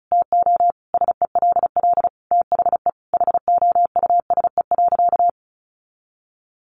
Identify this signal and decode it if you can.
Morse code, heard as a keyed tone, about 34 wpm, 700 Hz. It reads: TO SELL THE HOUSE.
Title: Sentences from Top 1500 Words 35wpm